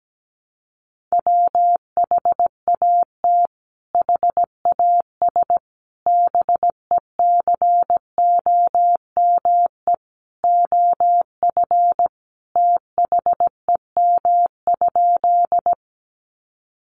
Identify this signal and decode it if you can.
Morse code, heard as a keyed tone, about 17 wpm, 700 Hz. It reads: WHAT HAS BECOME OF THEM?
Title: Sentences from Top 600 Words 17wpm